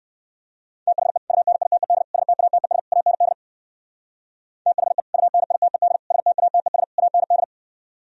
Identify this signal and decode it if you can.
Morse code, heard as a keyed tone, about 58 wpm, 700 Hz. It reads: THE UNITED STATES AND THE UNITED STATES AND